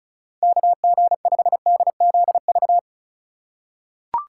Morse code, heard as a keyed tone, about 36 wpm, 700 Hz. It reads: KG5DZV
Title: USA Call Signs - 3x Character Spacing - 35wpm